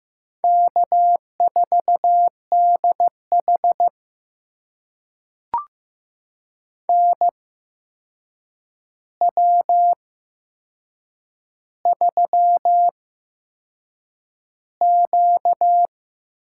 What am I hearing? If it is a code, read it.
Morse code, heard as a keyed tone, about 15 wpm, spaced as for 6 wpm, 700 Hz. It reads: K4DH NW3Q
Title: USA Call Signs - 4 Characters - 8x Character Spacing - 15wpm